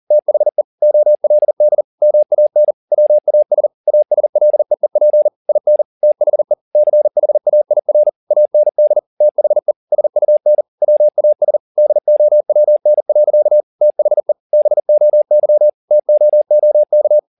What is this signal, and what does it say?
Morse code, heard as a keyed tone, about 40 wpm, 600 Hz. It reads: THE OLD MAN WAS ASLEEP IN THE CHAIR AND THE SUN WAS DOWN. THE BOY TOOK